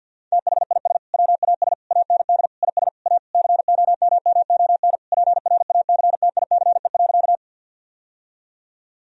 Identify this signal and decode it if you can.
Morse code, heard as a keyed tone, about 52 wpm, 700 Hz. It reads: THIS WAS AND IS A COMMON PRACTICE.
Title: Sentences from Top 1000 Words - Review Part 4 - 50wpm